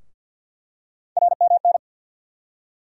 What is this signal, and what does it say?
Morse code, heard as a keyed tone, about 49 wpm, 700 Hz. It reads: RMN
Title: USA Call Sign Postfixes - 3 Letters 50wpm